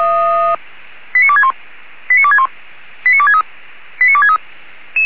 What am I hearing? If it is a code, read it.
Five-tone selective call (ZVEI), heard as quick runs of five tones, tones of about 70 ms. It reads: 89261 89261 89263 89263
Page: ZVEI